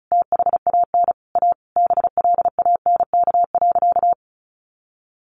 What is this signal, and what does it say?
Morse code, heard as a keyed tone, about 34 wpm, 700 Hz. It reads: THAN A BLANK.
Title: Sentences from Top 1500 Words 35wpm